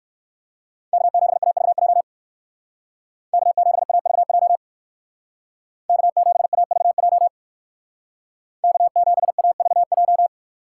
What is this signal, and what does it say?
Morse code, heard as a keyed tone, about 55 wpm, 700 Hz. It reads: K7AVJ K7AVJ K7AVJ K7AVJ
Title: Speed Racing - USA 1x3 Call Signs 45wpm